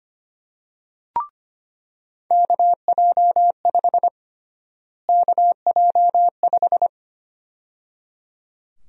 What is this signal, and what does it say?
Morse code, heard as a keyed tone, about 25 wpm, 700 Hz. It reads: KJ5 KJ5
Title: USA Call Sign Prefixes - Repeated 2x - 25wpm